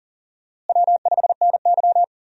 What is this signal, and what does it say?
Morse code, heard as a keyed tone, about 37 wpm, 700 Hz. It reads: W5NY
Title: USA Call Signs - 4 Characters - Repeated 3x - 40wpm